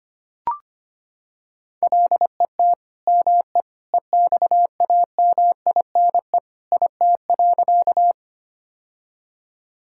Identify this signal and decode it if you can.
Morse code, heard as a keyed tone, about 25 wpm, 700 Hz. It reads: LET ME EXAMINE IT.
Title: Sentences from Top 1600 Words 25wpm